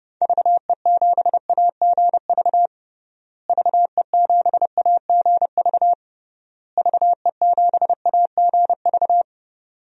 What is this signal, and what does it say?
Morse code, heard as a keyed tone, about 30 wpm, 700 Hz. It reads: VE7AGV VE7AGV VE7AGV
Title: DXCC-100 Call Signs - 6 Characters - Repeated 3x - 30wpm